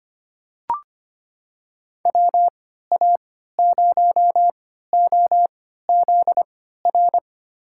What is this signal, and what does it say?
Morse code, heard as a keyed tone, about 25 wpm, spaced as for 14 wpm, 700 Hz. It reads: WA0OZR
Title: USA Call Signs - 6 Characters - 3x Character Spacing - 25wpm